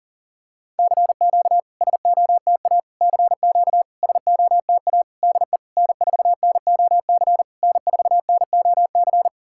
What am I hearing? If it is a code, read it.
Morse code, heard as a keyed tone, about 44 wpm, 700 Hz. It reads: CQ SOTA CQ SOTA DE N4NOC N4NOC
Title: SOTA QSO Practice 40wpm